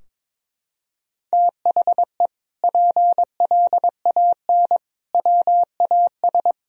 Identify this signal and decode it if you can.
Morse code, heard as a keyed tone, about 22 wpm, 700 Hz. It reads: THE PLAN WAS
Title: Sentences from Top 1000 Words - Review Part 3 - 22wpm